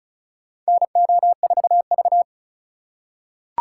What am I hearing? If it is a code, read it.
Morse code, heard as a keyed tone, about 34 wpm, 700 Hz. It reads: NO4V
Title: USA Call Signs - 2x Character Spacing - 35wpm